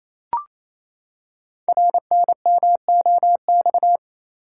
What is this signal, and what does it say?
Morse code, heard as a keyed tone, about 28 wpm, 700 Hz. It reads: RNMOX
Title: Code Groups - 5 Letters - 28wpm